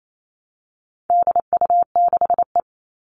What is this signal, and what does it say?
Morse code, heard as a keyed tone, about 28 wpm, 700 Hz. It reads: DU6E
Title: All Cut Numbers - 4-Digit 28wpm